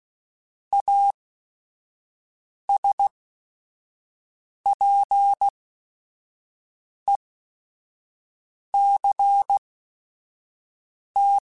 Morse code, heard as a keyed tone, about 16 wpm, spaced as for 5 wpm, 800 Hz. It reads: ASPECT
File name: Technical Subjects / Technical Subjects-mp3s / Sw Pwr Sup @5